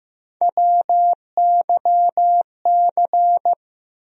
Morse code, heard as a keyed tone, about 15 wpm, 700 Hz. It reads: WYC
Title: USA Call Sign Postfixes - 3 Letters 15wpm